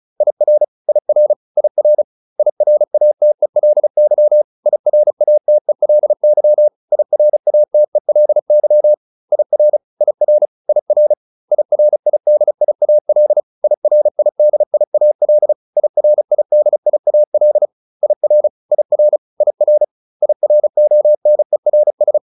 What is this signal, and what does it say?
Morse code, heard as a keyed tone, about 34 wpm, 600 Hz. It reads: IR IR IR IRATELY IRATELY IRATELY IR IR IR IRIDIAL IRIDIAL IRIDIAL IR IR IR IRONERS